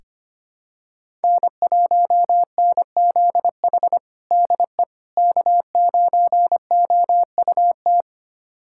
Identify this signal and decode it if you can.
Morse code, heard as a keyed tone, about 25 wpm, 700 Hz. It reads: N1NZH DE K9OUT
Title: USA Callsign DE Callsign - 5 Characters - 25wpm